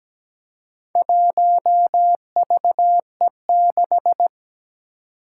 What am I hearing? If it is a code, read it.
Morse code, heard as a keyed tone, about 17 wpm, 700 Hz. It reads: 1VE6
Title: Four Challenging Letter-Number Combinations 17wpm